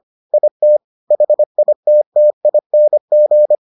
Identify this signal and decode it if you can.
Morse code, heard as a keyed tone, about 25 wpm, 600 Hz. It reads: IT HITTING